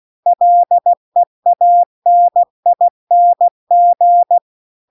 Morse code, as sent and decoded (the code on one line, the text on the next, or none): .-.. .- -. .. -. --.
LEANING